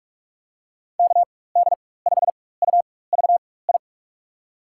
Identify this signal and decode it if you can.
Morse code, heard as a keyed tone, about 48 wpm, spaced as for 21 wpm, 700 Hz. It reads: KD5UVI
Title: USA Call Signs - 6 Characters - 4x Character Spacing - 45wpm